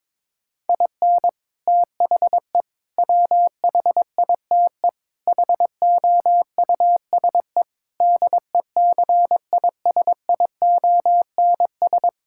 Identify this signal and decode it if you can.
Morse code, heard as a keyed tone, about 22 wpm, 700 Hz. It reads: IN THE WHITE HOUSE DECISIONS
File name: practice / 22wpm / Sentences with 4-Word Phrases - 22wpm